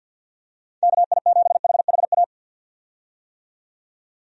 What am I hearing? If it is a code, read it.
Morse code, heard as a keyed tone, about 47 wpm, 700 Hz. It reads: KI7HHA
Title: USA Call Signs - 6 Characters - 50wpm